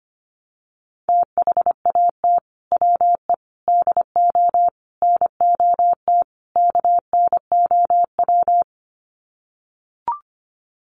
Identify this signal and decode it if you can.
Morse code, heard as a keyed tone, about 25 wpm, 700 Hz. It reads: THAT WE DO NOT KNOW
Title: Sets of 5 Words Encore2 25wpm